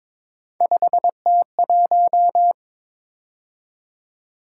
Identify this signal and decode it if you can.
Morse code, heard as a keyed tone, about 22 wpm, 700 Hz. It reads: 5T1
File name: practice / 22wpm / Three Letter-Number Combinations - 22wpm